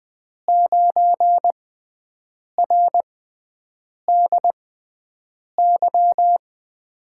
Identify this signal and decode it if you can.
Morse code, heard as a keyed tone, about 20 wpm, spaced as for 7 wpm, 700 Hz. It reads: 9RDY
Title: USA Call Signs - 5 Characters - 6x Character Spacing - 20wpm